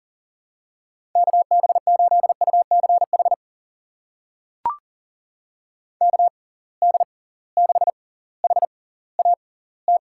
Morse code, heard as a keyed tone, about 39 wpm, spaced as for 20 wpm, 700 Hz. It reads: KB8UCH KD6HAT
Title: USA Call Signs - 6 Characters - 6x Character Spacing - 40wpm